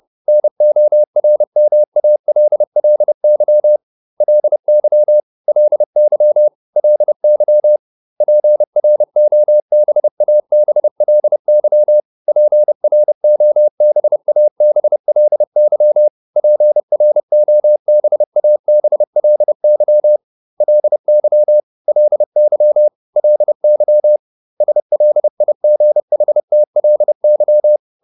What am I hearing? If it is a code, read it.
Morse code, heard as a keyed tone, about 30 wpm, 600 Hz. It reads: NORMALLY LY LY LY PROBABLY PROBABLY PROBABLY LY LY LY SLIGHTLY